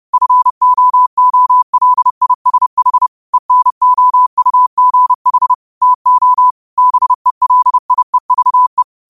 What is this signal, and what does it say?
Morse code, heard as a keyed tone, about 30 wpm, 1000 Hz. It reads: FOOLISH ENOUGH TO BELIEVE
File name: qotdfiles / 30wpm